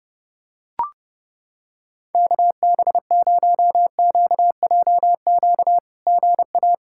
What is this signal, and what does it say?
Morse code, heard as a keyed tone, about 30 wpm, 700 Hz. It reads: KB0QJQ GA
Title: SOTA QSO Practice 30wpm